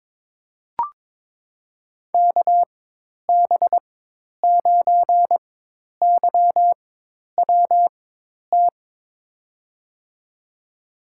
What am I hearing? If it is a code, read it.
Morse code, heard as a keyed tone, about 22 wpm, spaced as for 10 wpm, 700 Hz. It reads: KB9YWT
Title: USA Call Signs - 6 Characters - 4x Character Spacing - 22wpm